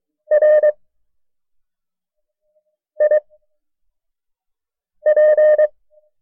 Morse code, heard as a keyed tone, about 23 wpm, spaced as for 4 wpm, 610 Hz. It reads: RIP
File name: FreeFiles / CW-practice / KB6MT-tape1-side2